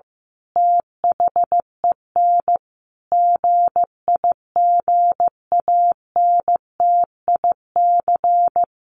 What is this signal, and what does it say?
Morse code, heard as a keyed tone, about 15 wpm, 700 Hz. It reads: THEN GIGANTIC
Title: A Sound Of Thunder 15wpm